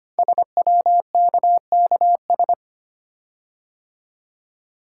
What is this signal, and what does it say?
Morse code, heard as a keyed tone, about 25 wpm, 700 Hz. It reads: SWKKS
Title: Code Groups - 5 Letters - 25wpm